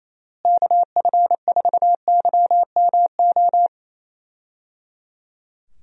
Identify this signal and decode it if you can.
Morse code, heard as a keyed tone, about 28 wpm, 700 Hz. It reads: KF4YMO
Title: USA Call Signs - 6 Characters - Repeated 3x - 28wpm